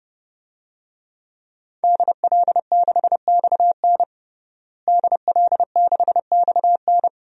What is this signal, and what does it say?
Morse code, heard as a keyed tone, about 32 wpm, 700 Hz. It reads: DL6XN DL6XN
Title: DXCC-100 Call Signs - 5 Characters - Repeated 3x - 30wpm